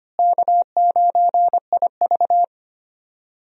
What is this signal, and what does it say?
Morse code, heard as a keyed tone, about 25 wpm, 700 Hz. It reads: K9IV
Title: USA Call Signs - 4 Characters - Repeated 3x - 25wpm